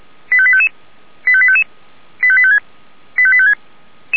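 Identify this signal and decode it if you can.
Five-tone selective call (ZVEI), heard as quick runs of five tones, tones of about 70 ms. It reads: 85855 85855 85856 85856
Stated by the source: ZVEI five-tone selective call 85853 received with an SDR.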